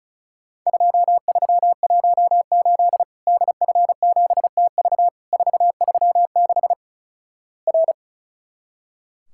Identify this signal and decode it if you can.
Morse code, heard as a keyed tone, about 36 wpm, 700 Hz. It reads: 2318 DF7TV 436 R
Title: Worked All Europe DX Contest - QTC Exchange - 35wpm